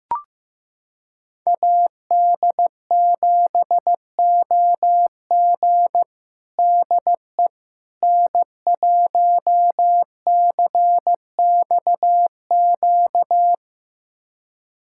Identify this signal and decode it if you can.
Morse code, heard as a keyed tone, about 16 wpm, 700 Hz. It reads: AD7OG DE N1CXQ
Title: USA Callsign DE Callsign - 5 Characters - 15wpm